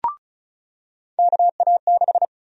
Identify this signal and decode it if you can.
Morse code, heard as a keyed tone, about 34 wpm, 700 Hz. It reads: KA6
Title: USA Call Sign Prefixes - 35wpm